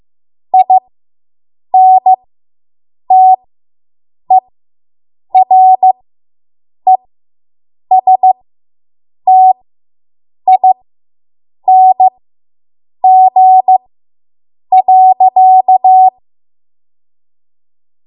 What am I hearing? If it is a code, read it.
Morse code, heard as a keyed tone, about 15 wpm, spaced as for 7 wpm, 750 Hz. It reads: INTERESTING.